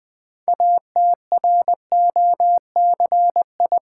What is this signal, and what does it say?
Morse code, heard as a keyed tone, about 20 wpm, 700 Hz. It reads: ATROCI
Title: Build A Word - 2 Letters At A Time - 8-Letter Words - 20wpm